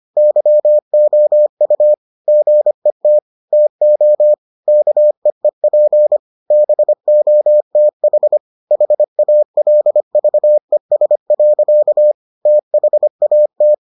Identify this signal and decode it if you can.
Morse code, heard as a keyed tone, about 25 wpm, 600 Hz. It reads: YOU GET TO KEEP BOTH HALVES. THAT